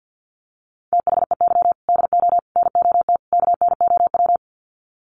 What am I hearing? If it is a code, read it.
Morse code, heard as a keyed tone, about 50 wpm, 700 Hz. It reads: THEY DO NOT KNOW
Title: Sets of 4 Words 50wpm